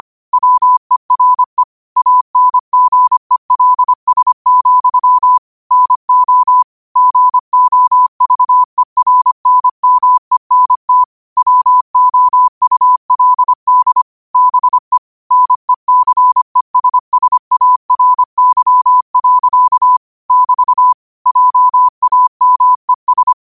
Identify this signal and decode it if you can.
Morse code, heard as a keyed tone, about 25 wpm, 1000 Hz. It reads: WERE ANGELS, NO GOVERNMENT WOULD BE NECESSARY. <BT> JAMES
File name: qotdfiles / 25wpm